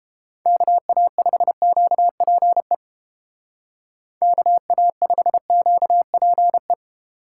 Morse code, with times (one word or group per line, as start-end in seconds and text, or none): 0.5-2.7 KA5QPE
4.2-6.7 KA5QPE